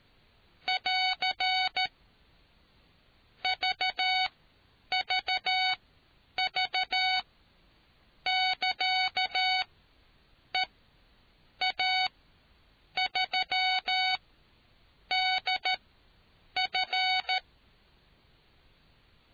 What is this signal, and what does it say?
Morse code, heard as a keyed tone, about 13 wpm, spaced as for 7 wpm, 750 Hz. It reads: <AR> VVV <KA> EA3DF